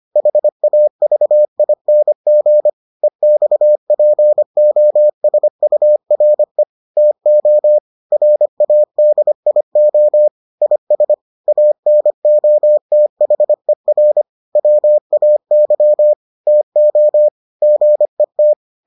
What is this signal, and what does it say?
Morse code, heard as a keyed tone, about 25 wpm, 600 Hz. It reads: HAVING EXPOSURE TO RADIO IS ANOTHER WAY TO GET